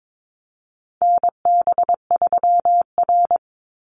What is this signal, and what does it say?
Morse code, heard as a keyed tone, about 22 wpm, 700 Hz. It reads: NB3R
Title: Call Signs Repeated 3x - 22wpm